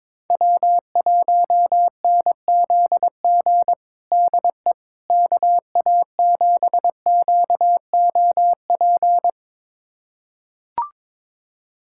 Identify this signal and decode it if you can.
Morse code, heard as a keyed tone, about 22 wpm, 700 Hz. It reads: W1NZG DE KA7QOP
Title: USA Callsign DE Callsign 22wpm